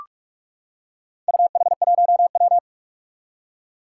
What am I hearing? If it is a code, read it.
Morse code, heard as a keyed tone, about 46 wpm, 700 Hz. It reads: UH1W
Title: Four Challenging Letter-Number Combinations 45wpm